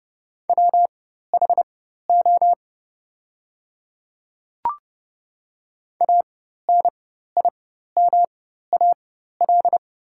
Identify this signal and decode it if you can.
Morse code, heard as a keyed tone, about 30 wpm, spaced as for 14 wpm, 700 Hz. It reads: WHO ANIMAL
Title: 4x Character Spacing - Top 200 Words - 30wpm